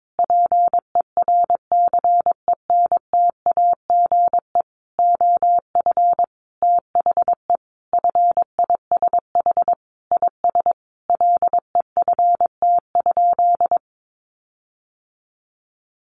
Morse code, heard as a keyed tone, about 22 wpm, 700 Hz. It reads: PERCENTAGE OF THE FISH IS LEFT?
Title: Sentences from Top 1300 Words 22wpm